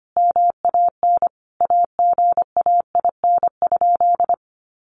Morse code, heard as a keyed tone, about 25 wpm, 700 Hz. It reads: MAN AGAIN?
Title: Sentences from Top 1400 Words 25wpm